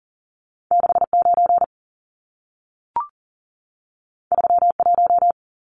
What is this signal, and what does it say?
Morse code, heard as a keyed tone, about 40 wpm, 700 Hz. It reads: 69 31
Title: Numbers - 2-Digit 40wpm